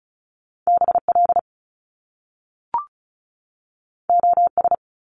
Morse code, heard as a keyed tone, about 36 wpm, 700 Hz. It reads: BL OS
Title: Top Two Letter Combinations in Words 35wpm